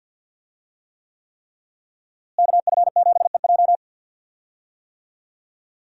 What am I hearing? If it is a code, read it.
Morse code, heard as a keyed tone, about 52 wpm, 700 Hz. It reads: KF7EJ
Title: USA Call Signs - 5 Characters - Repeated 3x - 50wpm